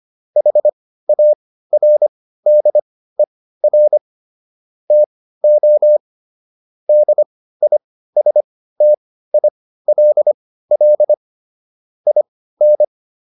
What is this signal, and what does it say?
Morse code, heard as a keyed tone, about 25 wpm, spaced as for 15 wpm, 600 Hz. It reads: HARDER TO DISTILL IN